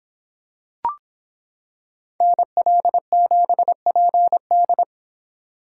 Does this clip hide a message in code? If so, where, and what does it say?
Yes: Morse code from 2.2 to 4.8 s, heard as a keyed tone, about 25 wpm, 700 Hz. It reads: NL7PD